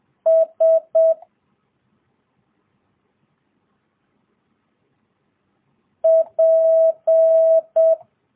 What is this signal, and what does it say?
Morse code, heard as a keyed tone, about 7 wpm, 650 Hz. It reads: S P
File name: jh8cbh / 2022cw